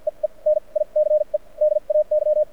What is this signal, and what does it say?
Morse code, heard as a keyed tone, about 54 wpm, spaced as for 40 wpm, 610 Hz. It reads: EE NICE DAY
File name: archived / CRSnet